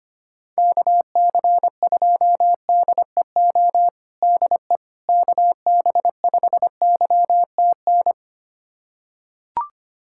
Morse code, heard as a keyed tone, about 25 wpm, 700 Hz. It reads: KC2DEO DE KB5YTN
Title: USA Callsign DE Callsign 25wpm